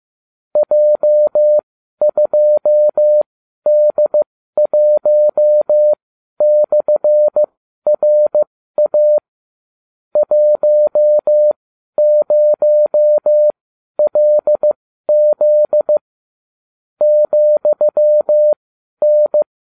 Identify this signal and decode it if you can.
Morse code, heard as a keyed tone, about 15 wpm, spaced as for 12 wpm, 600 Hz. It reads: J2D1/RA 10LZ ,N